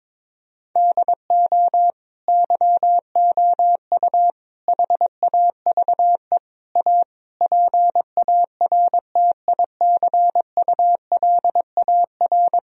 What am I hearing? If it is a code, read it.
Morse code, heard as a keyed tone, about 22 wpm, 700 Hz. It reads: DO YOU HAVE A PARTICULAR
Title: Sentences from Top 1000 Words - Review Part 4 - 22wpm